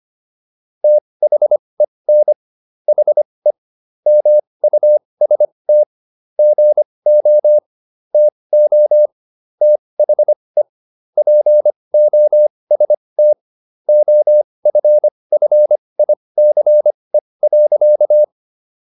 Morse code, heard as a keyed tone, about 25 wpm, spaced as for 20 wpm, 600 Hz. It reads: THEN HE MUST GO TO THE POST OFFICE.